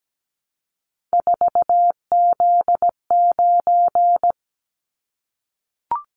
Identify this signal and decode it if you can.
Morse code, heard as a keyed tone, about 17 wpm, 700 Hz. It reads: 4Z9
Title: Callsign Prefixes for Top 50 DXCC - 17wpm